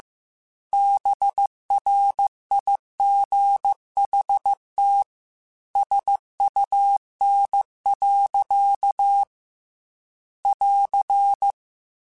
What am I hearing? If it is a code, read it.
Morse code, heard as a keyed tone, about 15 wpm, 800 Hz. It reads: BRIGHT SUN. <AR>
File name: Weather Sets / Weather Sets-mp3s / Weather 5 5 @13